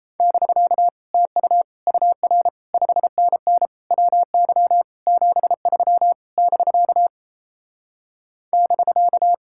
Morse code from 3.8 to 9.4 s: WY 73 <BK> <BK>

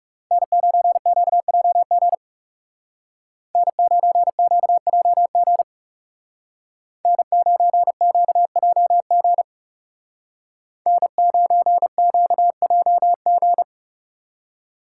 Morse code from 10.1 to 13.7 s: N9QJG

